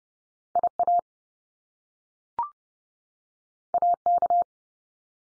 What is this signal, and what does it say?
Morse code, heard as a keyed tone, about 30 wpm, 700 Hz. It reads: IA AK
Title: US States 30WPM